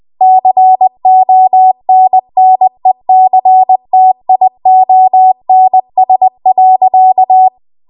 Morse code, heard as a keyed tone, about 20 wpm, 750 Hz. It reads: CONNECTIONS.
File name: Archive / 20 WPM / 20WPM